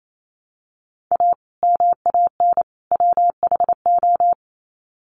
Morse code, heard as a keyed tone, about 28 wpm, 700 Hz. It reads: A MAN WHO
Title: Sets of 3 Words 28wpm